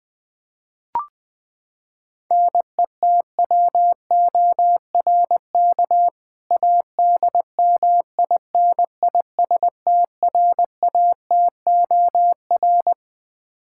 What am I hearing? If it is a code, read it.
Morse code, heard as a keyed tone, about 20 wpm, 700 Hz. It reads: NETWORK ADMINISTRATOR
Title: K1USN SST QSO Practice 20wpm